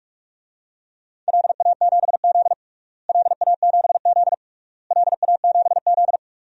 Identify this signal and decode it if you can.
Morse code, heard as a keyed tone, about 46 wpm, 690 Hz. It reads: LA7Z LA7Z LA7Z
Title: DXCC-100 Call Signs - 4 Characters - Repeated 3x - 45wpm